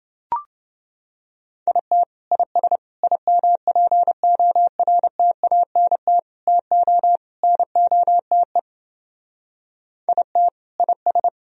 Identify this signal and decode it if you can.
Morse code, heard as a keyed tone, about 29 wpm, 700 Hz. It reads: IT IS IMPORTANT TO NOTE IT IS